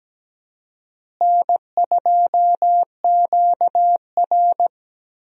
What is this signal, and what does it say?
Morse code, heard as a keyed tone, about 17 wpm, 700 Hz. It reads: N2QR
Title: USA Call Signs - 4 Characters - Repeated 3x - 17wpm